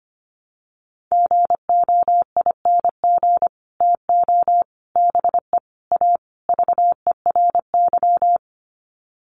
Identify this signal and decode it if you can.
Morse code, heard as a keyed tone, about 25 wpm, 700 Hz. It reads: GOING TO BE A VERY
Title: Sets of 5 Words Encore 25wpm